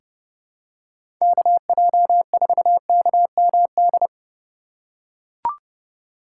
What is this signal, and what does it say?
Morse code, heard as a keyed tone, about 32 wpm, 700 Hz. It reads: KJ4KMD
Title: USA Call Signs - 6 Characters - 3x Character Spacing - 30wpm